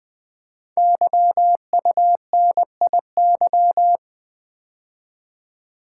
Morse code, heard as a keyed tone, about 20 wpm, 700 Hz. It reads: YUNIY
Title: Code Groups - 5 Letters - 20wpm